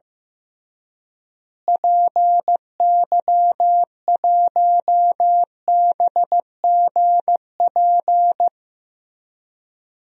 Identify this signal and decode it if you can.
Morse code, heard as a keyed tone, about 15 wpm, 700 Hz. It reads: PY1BGP
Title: DXCC-100 Call Signs - 6 Characters - Repeated 3x - 15wpm